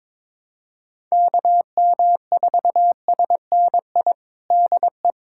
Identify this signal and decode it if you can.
Morse code, heard as a keyed tone, about 22 wpm, 700 Hz. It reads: KM4SNI DE
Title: USA Callsign DE Callsign 22wpm